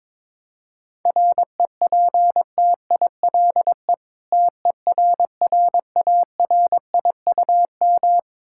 Morse code, heard as a keyed tone, about 22 wpm, 700 Hz. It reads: REPTILE TERRARIUM
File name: Difficult Words with Hints - 22wpm